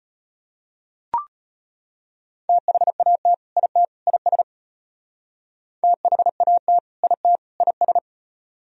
Morse code, heard as a keyed tone, about 32 wpm, 700 Hz. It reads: THAT IT IS THAT IT IS